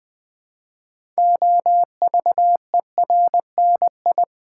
Morse code, heard as a keyed tone, about 20 wpm, 700 Hz. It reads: OVERNI
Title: Build A Word - 3 Letters At A Time - 9-Letter Words - 20wpm